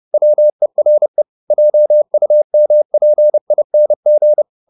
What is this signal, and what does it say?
Morse code, heard as a keyed tone, about 30 wpm, 600 Hz. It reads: WERE JUMPING
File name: the old man and the sea / the old man and the sea 1 30wpm